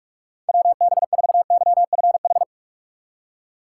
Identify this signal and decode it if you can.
Morse code, heard as a keyed tone, about 46 wpm, 700 Hz. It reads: WB4YFH
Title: USA Call Signs - 2x Character Spacing - 45wpm